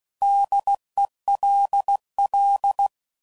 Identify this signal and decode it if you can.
Morse code, heard as a keyed tone, about 16 wpm, 800 Hz. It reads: DELL
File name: Prefixes / By First Letter / Ds